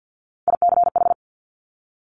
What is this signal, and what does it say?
Morse code, heard as a keyed tone, about 50 wpm, 700 Hz. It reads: ICH
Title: Top Three Letter Combinations in Words 50wpm